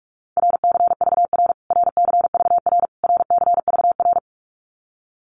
Morse code, heard as a keyed tone, about 44 wpm, 700 Hz. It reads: RCVR RCVR RCVR
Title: Abbreviations Repeated 3x - 45wpm